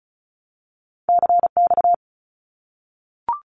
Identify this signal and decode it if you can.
Morse code, heard as a keyed tone, about 34 wpm, 700 Hz. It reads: CX
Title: Abbreviations Repeated 3x - 35wpm